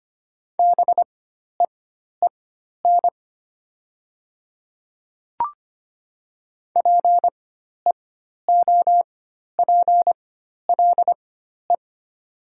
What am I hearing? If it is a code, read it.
Morse code, heard as a keyed tone, about 25 wpm, spaced as for 12 wpm, 700 Hz. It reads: BEEN PEOPLE
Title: 4x Character Spacing - Top 100 Words - 25wpm